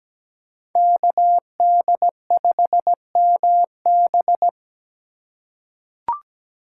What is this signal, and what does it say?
Morse code, heard as a keyed tone, about 17 wpm, 700 Hz. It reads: KD5MB
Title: USA Call Signs - 5 Characters - 2x Character Spacing - 17wpm